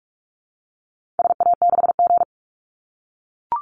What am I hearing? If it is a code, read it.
Morse code, heard as a keyed tone, about 44 wpm, 700 Hz. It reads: SA6G
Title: Call Signs - 45wpm